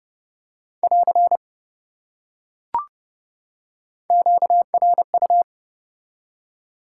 Morse code, heard as a keyed tone, about 27 wpm, 700 Hz. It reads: <AR> QRU